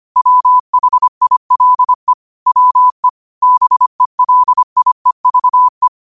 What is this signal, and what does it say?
Morse code, heard as a keyed tone, about 25 wpm, 1000 Hz. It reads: WHILE WE BELIEVE
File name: qotdfiles / 25wpm